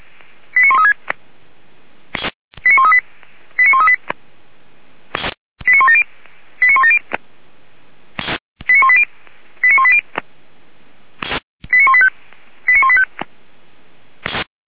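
Five-tone selective call (ZVEI), heard as quick runs of five tones, tones of about 80 ms. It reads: 89136 89138 89138 89170 89170 89180 89180 89185 89185
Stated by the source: ZVEI five-tone selective call received by SDR, from the Feuerwehr Orlamünde audio server.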